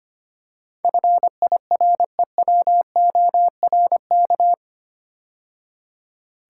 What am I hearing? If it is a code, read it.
Morse code, heard as a keyed tone, about 25 wpm, 700 Hz. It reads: FIREWORK
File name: 8 Letter Words - First 3 Letters Given - 25wpm